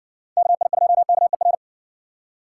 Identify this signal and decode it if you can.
Morse code, heard as a keyed tone, about 60 wpm, 700 Hz. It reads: KI2KER